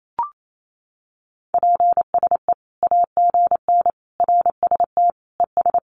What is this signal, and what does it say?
Morse code, heard as a keyed tone, about 28 wpm, 700 Hz. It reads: PSE AGN RST ES
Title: Common QSO Phrases 28wpm